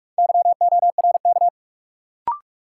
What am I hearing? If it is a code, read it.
Morse code, heard as a keyed tone, about 44 wpm, 700 Hz. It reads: YORK
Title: Sets of 2 Words - New Word Drill - 45wpm